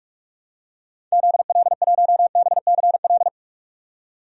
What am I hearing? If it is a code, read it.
Morse code, heard as a keyed tone, about 42 wpm, 700 Hz. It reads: ZL1BCL
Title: DXCC-100 Call Signs - 6 Characters - 45wpm